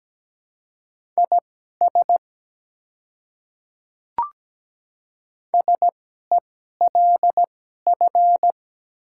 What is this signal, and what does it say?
Morse code, heard as a keyed tone, about 17 wpm, spaced as for 12 wpm, 690 Hz. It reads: IS SELF
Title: 2x Character Spacing - Top 300 Words - 17wpm